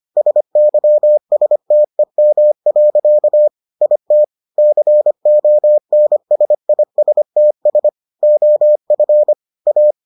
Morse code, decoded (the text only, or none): SYSTEM. IT CONSISTS OF A